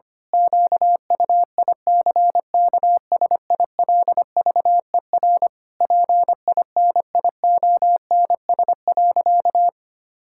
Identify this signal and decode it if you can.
Morse code, heard as a keyed tone, about 25 wpm, 700 Hz. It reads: QUICKSILVER PINIONS.
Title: Pangram Challenge v2 - 25wpm